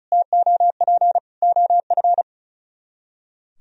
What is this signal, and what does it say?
Morse code, heard as a keyed tone, about 43 wpm, 700 Hz. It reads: TOP OF
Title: Speed-Racing - Sets of 3 Words 35wpm